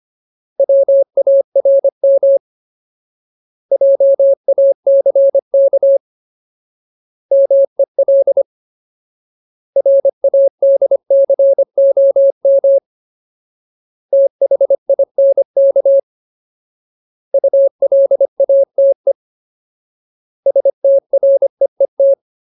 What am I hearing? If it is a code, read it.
Morse code, heard as a keyed tone, about 25 wpm, 550 Hz. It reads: WARM JACK MEL RADCOM THINK ULATE STREET
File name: Ultimate Word List / 25WPM Ultimate